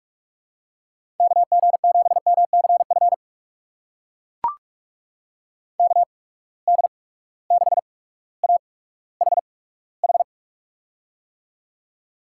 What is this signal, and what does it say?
Morse code, heard as a keyed tone, about 44 wpm, spaced as for 19 wpm, 700 Hz. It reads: KG7MCF KD6AHH
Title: USA Call Signs - 6 Characters - 8x Character Spacing - 45wpm